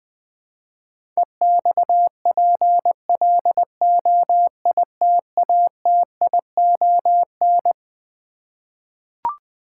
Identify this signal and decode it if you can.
Morse code, heard as a keyed tone, about 20 wpm, 700 Hz. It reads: EXPLOITATION
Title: Twelve Letter Words - 20wpm